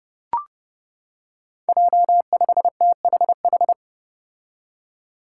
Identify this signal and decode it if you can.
Morse code, heard as a keyed tone, about 30 wpm, 700 Hz. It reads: J5THH
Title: Five Challenging Letter-Number Combinations 30wpm